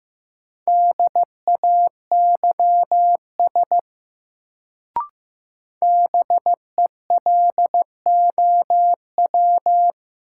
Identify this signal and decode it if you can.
Morse code, heard as a keyed tone, about 15 wpm, 700 Hz. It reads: DAYS BELOW
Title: Sentences from Top 1000 Words - Review Part 5 - New Word Drill - 15wpm